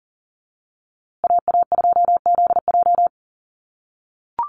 Morse code, heard as a keyed tone, about 40 wpm, 700 Hz. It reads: AA2ZJ
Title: USA Call Signs Repeated 2x 40wpm